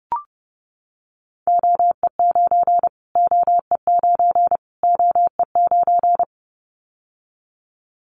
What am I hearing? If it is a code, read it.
Morse code, heard as a keyed tone, about 30 wpm, 690 Hz. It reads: OE9 OE9 OE9
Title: Callsign Prefixes for Top 25 DXCC - Repeated 3x - 30wpm